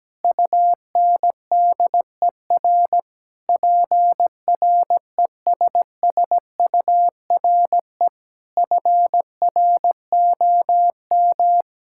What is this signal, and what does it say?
Morse code, heard as a keyed tone, about 17 wpm, 700 Hz. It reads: UNDER PRESSURE FROM
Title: Sentences from Top 1000 Words - Review Part 6 - 17wpm